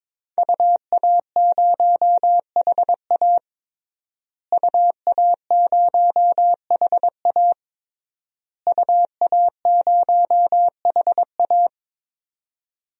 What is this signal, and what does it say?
Morse code, heard as a keyed tone, about 22 wpm, 700 Hz. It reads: UA0HA UA0HA UA0HA
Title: DXCC-100 Call Signs - 5 Characters - Repeated 3x - 22wpm